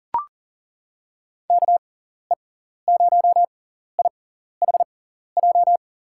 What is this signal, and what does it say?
Morse code, heard as a keyed tone, about 40 wpm, spaced as for 14 wpm, 700 Hz. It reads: KE0IHJ
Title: USA Call Signs - 6 Characters - 6x Character Spacing - 40wpm